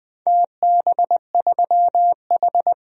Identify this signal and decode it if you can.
Morse code, heard as a keyed tone, about 20 wpm, 700 Hz. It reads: TB3H
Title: DXCC-100 Call Signs - 4 Characters - Repeated 3x - 20wpm